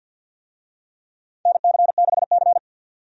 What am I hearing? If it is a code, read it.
Morse code, heard as a keyed tone, about 52 wpm, 700 Hz. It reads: NC6C
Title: USA Call Signs - 4 Characters - 50wpm